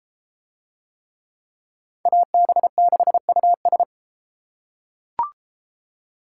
Morse code, heard as a keyed tone, about 33 wpm, 700 Hz. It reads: AB6US